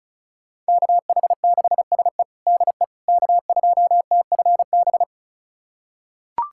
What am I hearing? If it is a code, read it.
Morse code, heard as a keyed tone, about 36 wpm, 700 Hz. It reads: KH6SE DE K2TFB
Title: USA Callsign DE Callsign - 5 Characters - 35wpm